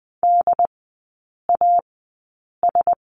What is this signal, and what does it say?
Morse code, heard as a keyed tone, about 20 wpm, spaced as for 8 wpm, 700 Hz. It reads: DAS
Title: 067 Lesson 12e - Letter d - Three Characters 20wpm